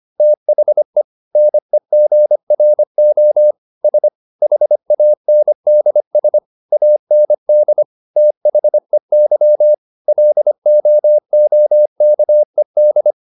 Morse code, heard as a keyed tone, about 25 wpm, 600 Hz. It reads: THE NEGRO S HANDS AND THEY LOOKED